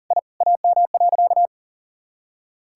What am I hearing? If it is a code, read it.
Morse code, heard as a keyed tone, about 40 wpm, 700 Hz. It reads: I AM.
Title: Sentences from Top 1000 Words - Review Part 5 - 40wpm